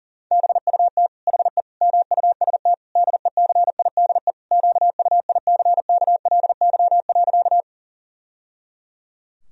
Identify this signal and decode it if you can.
Morse code, heard as a keyed tone, about 40 wpm, 700 Hz. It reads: BUT HE MUST DECIDE QUICKLY.
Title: Sentences from Top 1000 Words - Review Part 3 - 40wpm